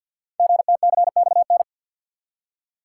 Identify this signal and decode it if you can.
Morse code, heard as a keyed tone, about 48 wpm, 690 Hz. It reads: GTCXN
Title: Code Groups - 5 Letters - 50wpm